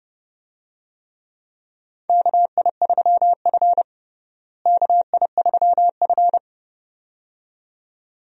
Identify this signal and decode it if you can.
Morse code, heard as a keyed tone, about 31 wpm, 700 Hz. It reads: KI3F KI3F
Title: USA Call Signs - 4 Characters - Repeated 2x - 30wpm